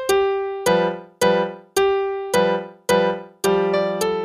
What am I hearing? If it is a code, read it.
There is no code.